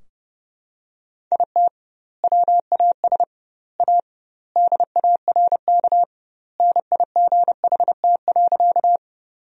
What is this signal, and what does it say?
Morse code, heard as a keyed tone, about 30 wpm, 700 Hz. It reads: IT WAS A DARK NIGHT.
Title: 2x Word-Spacing - Sentences from Top 500 Words - 30wpm